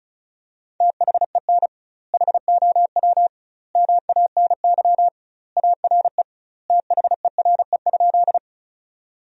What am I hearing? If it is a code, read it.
Morse code, heard as a keyed tone, about 36 wpm, 700 Hz. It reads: THEN HOW MANY ARE THERE?
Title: 2x Word-Spacing - Sentences from Top 100 Words - 35wpm